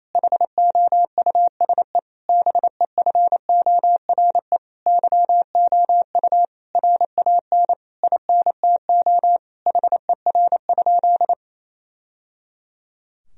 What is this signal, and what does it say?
Morse code, heard as a keyed tone, about 28 wpm, 700 Hz. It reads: HOUSE BEFORE YOU RAN INTO HER?
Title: Sentences from Top 1000 Words - Review Part 3 - 28wpm